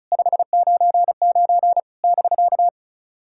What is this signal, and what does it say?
Morse code, heard as a keyed tone, about 34 wpm, 700 Hz. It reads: 599 <BK>
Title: SOTA QSO Practice 35wpm